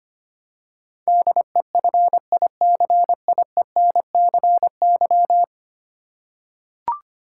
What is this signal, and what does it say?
Morse code, heard as a keyed tone, about 25 wpm, 700 Hz. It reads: DEFICIENCY
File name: Ten Letter Words - 25wpm